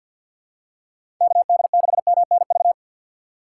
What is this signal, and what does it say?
Morse code, heard as a keyed tone, about 47 wpm, 700 Hz. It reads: KD6MNV
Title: USA Call Signs - 6 Characters - 3x Character Spacing - 50wpm